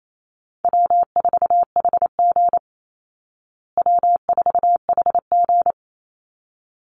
Morse code, heard as a keyed tone, about 28 wpm, 700 Hz. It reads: W4HG W4HG
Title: Easy Call Signs Repeated 2x - 28wpm